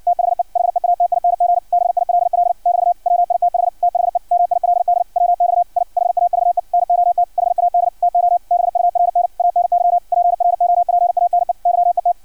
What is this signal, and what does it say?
Morse code, heard as a keyed tone, about 56 wpm, 700 Hz. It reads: THE HEATING BILL <BT> WITH THE MILD WX I HAVE NOT HAD TO BURN ANY PROPANE YET
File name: archived / CRSnet